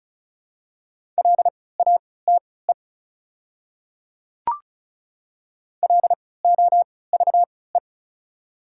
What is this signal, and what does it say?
Morse code, heard as a keyed tone, about 34 wpm, spaced as for 20 wpm, 690 Hz. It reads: LATE LOVE
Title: 3x Character Spacing - Top 500 Words - 35wpm